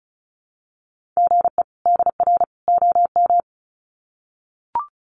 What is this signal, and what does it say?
Morse code, heard as a keyed tone, about 34 wpm, 700 Hz. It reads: GE DR OM
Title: Common QSO Phrases 35wpm